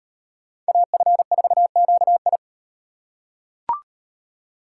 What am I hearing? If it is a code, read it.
Morse code, heard as a keyed tone, about 38 wpm, 700 Hz. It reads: AF4QI